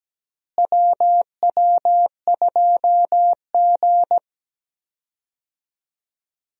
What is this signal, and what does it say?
Morse code, heard as a keyed tone, about 17 wpm, 700 Hz. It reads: WW2G
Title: USA Call Signs - 4 Characters - Repeated 2x - 17wpm